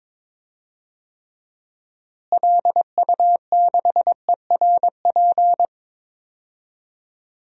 Morse code, heard as a keyed tone, about 22 wpm, 700 Hz. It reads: LU6ERP
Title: DXCC-100 Call Signs - 6 Characters - Repeated 2x - 22wpm